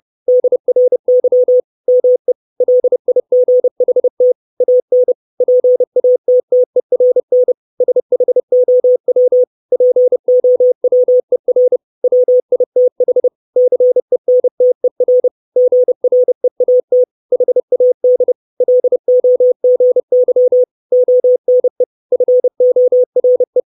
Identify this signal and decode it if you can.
Morse code, heard as a keyed tone, about 30 wpm, 500 Hz. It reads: DRY ME LIGHT AN PATTERN SHOW POWER WITH CENTER GREAT HAD LOGY ONE FORE
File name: 632 Words 30WPM